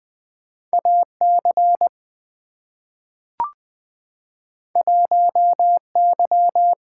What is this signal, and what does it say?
Morse code, heard as a keyed tone, about 20 wpm, 700 Hz. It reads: AC 1Y
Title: Two Letter-Number Combinations - 20wpm